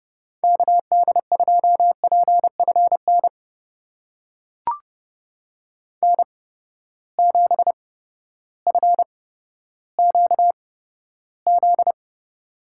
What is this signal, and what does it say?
Morse code, heard as a keyed tone, about 30 wpm, spaced as for 14 wpm, 700 Hz. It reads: KD2PFN N7FQZ
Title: USA Call Signs - 8x Character Spacing - 30wpm